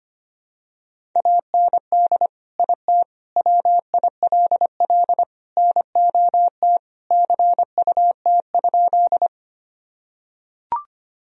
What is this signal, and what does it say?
Morse code, heard as a keyed tone, about 25 wpm, 700 Hz. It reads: AND IT WILL NOT CUT?
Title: Sentences from Top 1000 Words - Review Part 1 - 25wpm